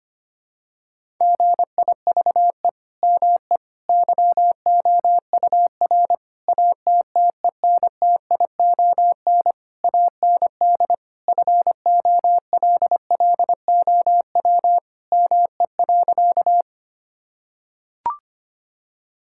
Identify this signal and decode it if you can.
Morse code, heard as a keyed tone, about 25 wpm, 700 Hz. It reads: GIVE ME YOUR ATTENTION AND FOLLOW ME.
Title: Sentences from Top 1000 Words - Review Part 6 - 25wpm